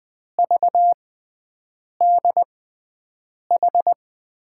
Morse code, heard as a keyed tone, about 20 wpm, spaced as for 7 wpm, 700 Hz. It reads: VDH
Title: USA Call Signs - 5 Characters - 6x Character Spacing - 20wpm